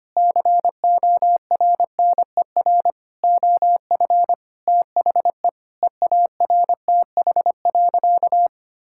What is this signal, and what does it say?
Morse code, heard as a keyed tone, about 25 wpm, 700 Hz. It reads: CORNER OF THE EARTH.
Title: Sentences from Top 1600 Words 25wpm